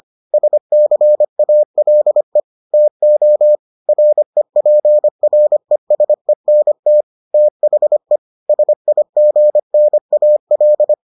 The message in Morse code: SCALE TO REPRESENT THE SIGNAL

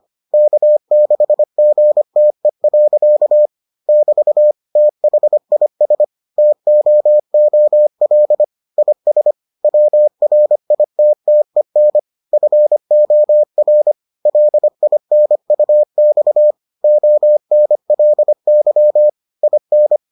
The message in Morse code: K6GTE. <BT> THIS TOOL IS WRITTEN FOR LINUX ONLY IN

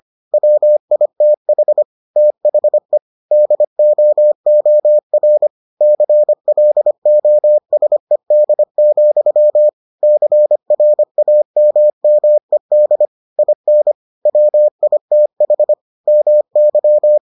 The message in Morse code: WITH THE DOOR CLOSED, CRAMMED IN WITH MY

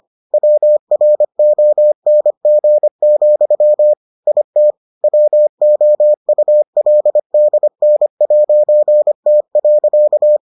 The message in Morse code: WRONG, IT WOULDN'T.

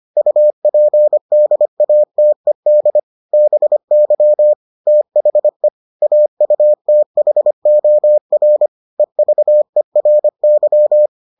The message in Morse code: UPDATED BY THE AUTHOR EVERY

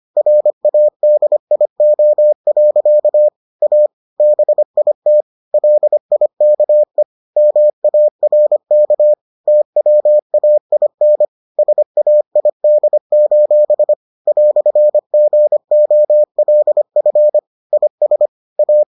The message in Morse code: RADIO. A BIT LIKE MARK TWAIN SAID: "GOLF IS A

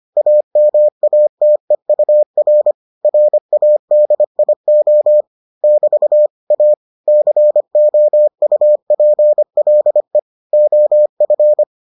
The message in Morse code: AMATEUR RADIO <BT> A COUPLE OF

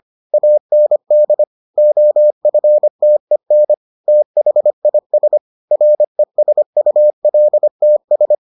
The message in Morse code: AND OFTEN THIS RESULTS

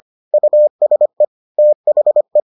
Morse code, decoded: USE THE

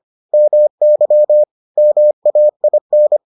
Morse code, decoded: MY MAIN

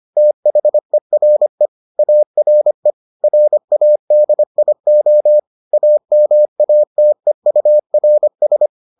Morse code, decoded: THERE ARE RADIO AMATEURS

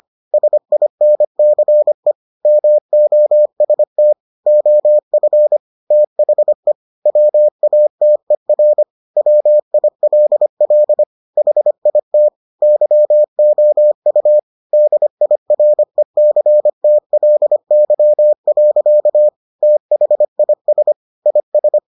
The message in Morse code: SINCE MOST OF THE WATER WILL HIT YOU DIRECTLY. THIS IS

